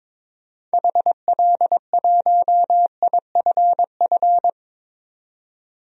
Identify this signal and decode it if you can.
Morse code, heard as a keyed tone, about 22 wpm, 700 Hz. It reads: HL1IFF
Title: DXCC-100 Call Signs - 6 Characters - 22wpm